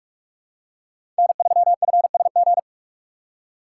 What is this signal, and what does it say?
Morse code, heard as a keyed tone, about 43 wpm, 700 Hz. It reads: N3FSG
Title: USA Call Signs - 4x Character Spacing - 45wpm